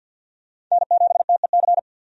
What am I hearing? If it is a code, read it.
Morse code, heard as a keyed tone, about 51 wpm, 700 Hz. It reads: N7TEC
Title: USA Call Signs - 5 Characters - 50wpm